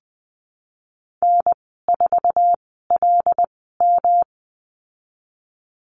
Morse code, heard as a keyed tone, about 20 wpm, spaced as for 15 wpm, 700 Hz. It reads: N4LM
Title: 200 Lesson 32g - Letter k - Call Signs 20wpm